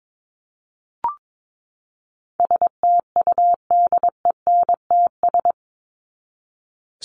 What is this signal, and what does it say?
Morse code, heard as a keyed tone, about 22 wpm, 700 Hz. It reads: STUDENTS
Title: ES:S Suffix - Top 100 Words with Base Words 22wpm